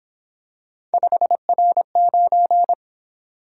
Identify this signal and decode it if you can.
Morse code, heard as a keyed tone, about 26 wpm, 700 Hz. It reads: 5R9
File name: Speed-Racing - Three Letter-Number Combinations - 22wpm